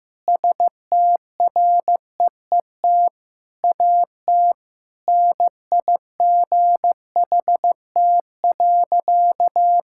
Morse code, heard as a keyed tone, about 15 wpm, 700 Hz. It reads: STREET AT NIGHT.
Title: Sentences from Top 1000 Words - Review Part 6 - 15wpm